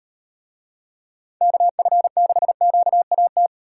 Morse code, heard as a keyed tone, about 36 wpm, 700 Hz. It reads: KF6QAT